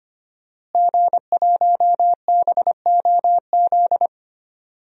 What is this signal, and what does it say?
Morse code, heard as a keyed tone, about 25 wpm, 700 Hz. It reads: G1BOZ
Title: DXCC-100 Call Signs - 5 Characters - Repeated 3x - 25wpm